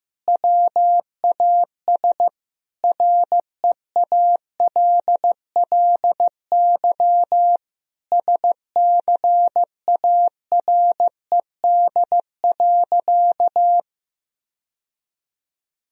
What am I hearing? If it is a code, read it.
Morse code, heard as a keyed tone, about 15 wpm, 700 Hz. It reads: WAS REALLY SCARED.
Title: Sentences from Top 1000 Words - Review Part 4 - 15wpm